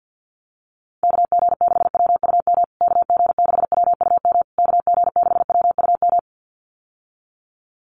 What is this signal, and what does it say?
Morse code, heard as a keyed tone, about 50 wpm, 700 Hz. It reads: KG6WUM KG6WUM KG6WUM
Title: USA Call Signs Repeated 3x 50wpm